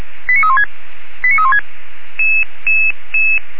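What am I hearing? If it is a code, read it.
Five-tone selective call (ZVEI), heard as quick runs of five tones, tones of about 70 ms. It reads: 89316 89316
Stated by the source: ZVEI five-tone selective call received on SDR.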